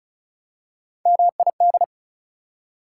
Morse code, heard as a keyed tone, about 36 wpm, 690 Hz. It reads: MID
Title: Prefixes and Suffixes 35wpm